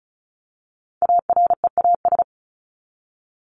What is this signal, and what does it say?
Morse code, heard as a keyed tone, about 34 wpm, 700 Hz. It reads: AREAS
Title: ES:S Suffix - Top 100 Words with Base Words 35wpm